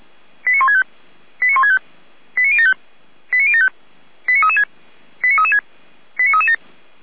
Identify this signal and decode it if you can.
Five-tone selective call (ZVEI), heard as quick runs of five tones, tones of about 70 ms. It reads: 89165 89165 89075 89075 89336 89336 89337